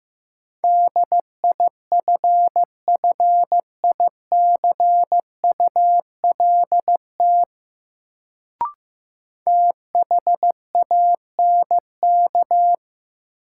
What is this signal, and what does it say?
Morse code, heard as a keyed tone, about 15 wpm, 700 Hz. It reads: DIFFICULT THANK
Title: Sentences from Top 600 Words - New Word Drill - 15wpm